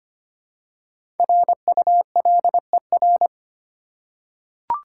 Morse code, heard as a keyed tone, about 25 wpm, 690 Hz. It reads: RULER
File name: practice / 25wpm / Easy Words with Hints - 25wpm